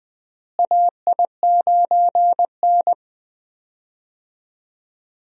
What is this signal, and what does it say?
Morse code, heard as a keyed tone, about 19 wpm, 700 Hz. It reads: AI9N
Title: USA Call Signs - 4 Characters - Repeated 3x - 20wpm